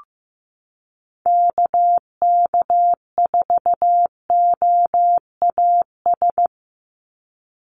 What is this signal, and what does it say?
Morse code, heard as a keyed tone, about 15 wpm, 700 Hz. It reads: KK4OAS
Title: USA Call Signs 15wpm